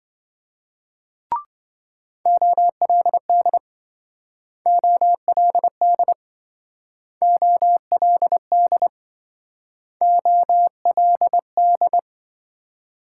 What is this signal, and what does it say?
Morse code, heard as a keyed tone, about 26 wpm, 700 Hz. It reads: OLD OLD OLD OLD